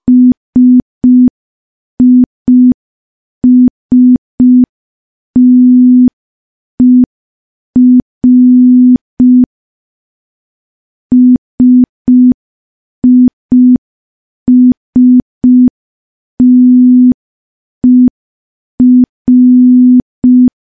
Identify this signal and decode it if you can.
Morse code, heard as a keyed tone, about 5 wpm, 250 Hz. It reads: SISTER SISTER
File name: morse / morse-training-05WPM